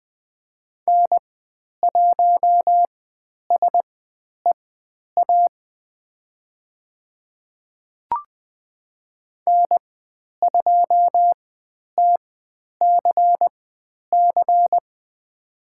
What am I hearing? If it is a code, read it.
Morse code, heard as a keyed tone, about 20 wpm, spaced as for 10 wpm, 700 Hz. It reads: N1SEA N2TCC
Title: 102 Lesson 17g - Letter u - Call Signs 20wpm